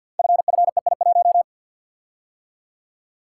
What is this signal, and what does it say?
Morse code, heard as a keyed tone, about 48 wpm, 700 Hz. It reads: FFEI1
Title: Code Groups - 5 Letter:Numbers - 50wpm